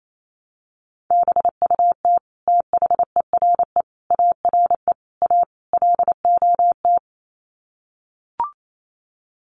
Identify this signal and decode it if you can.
Morse code, heard as a keyed tone, about 28 wpm, 700 Hz. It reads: BUT THERE ARE A LOT
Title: Sets of 5 Words Encore2 28wpm